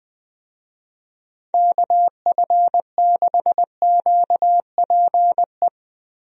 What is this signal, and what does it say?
Morse code, heard as a keyed tone, about 20 wpm, 700 Hz. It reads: KF6QPE
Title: USA Call Signs - 6 Characters - Repeated 3x - 20wpm